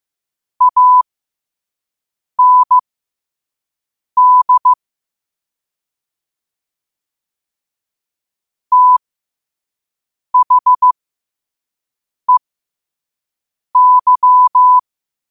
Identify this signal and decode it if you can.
Morse code, heard as a keyed tone, about 15 wpm, spaced as for 5 wpm, 1000 Hz. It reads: AND THEY
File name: qotdfiles / 5wpm